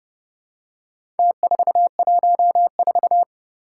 Morse code, heard as a keyed tone, about 29 wpm, 700 Hz. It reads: T414
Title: Four Challenging Letter-Number Combinations 30wpm